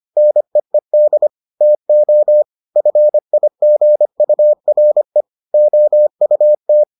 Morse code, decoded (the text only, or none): NEED TO FIGURE OUT